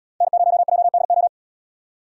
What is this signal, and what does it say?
Morse code, heard as a keyed tone, about 77 wpm, 700 Hz. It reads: N0JDO